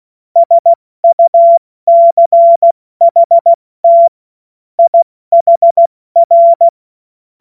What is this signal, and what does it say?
Morse code, heard as a keyed tone, about 16 wpm, spaced as for 14 wpm, 680 Hz. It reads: SUCHT IHR
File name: Morsekurs / Witze / Witz 20. Januar 2026 70 F80